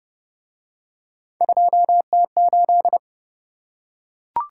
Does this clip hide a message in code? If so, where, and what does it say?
Yes: Morse code from 1.4 to 3.0 s, heard as a keyed tone, about 29 wpm, 700 Hz. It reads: 2T8